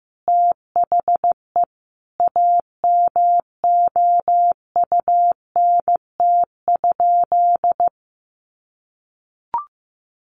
Morse code, heard as a keyed tone, about 15 wpm, 700 Hz. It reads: THE AMOUNT?
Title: Sentences from Top 700 Words 15wpm